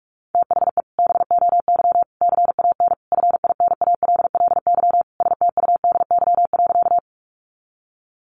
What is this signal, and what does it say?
Morse code, heard as a keyed tone, about 44 wpm, 700 Hz. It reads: THE BOY CAN FINALLY STUDY.
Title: Additional Easy Phrases 45wpm